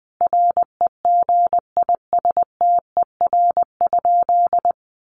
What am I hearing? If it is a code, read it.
Morse code, heard as a keyed tone, about 20 wpm, 700 Hz. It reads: REGISTER?
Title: Sentences from Top 1200 Words 20wpm